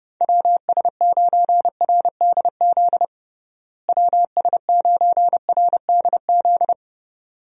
Morse code, heard as a keyed tone, about 32 wpm, 700 Hz. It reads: WS9RDZ WS9RDZ
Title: USA Call Signs - 6 Characters - Repeated 2x - 30wpm